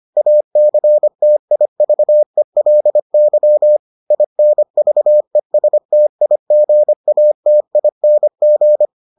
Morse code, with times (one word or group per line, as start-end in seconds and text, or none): 0.2-3.8 ACTIVELY
4.1-8.9 INVESTIGATING